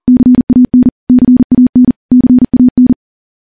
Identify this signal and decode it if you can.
Morse code, heard as a keyed tone, about 40 wpm, 250 Hz. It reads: CAN CAN CAN
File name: morse-training-40WPM